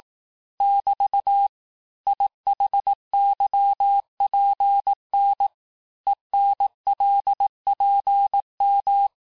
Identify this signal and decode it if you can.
Morse code, heard as a keyed tone, about 18 wpm, 800 Hz. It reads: <BT> IHYPN ENLPM
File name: cours-du-lundi files / cours-1588-texte-chiffre-vitesse-17MPM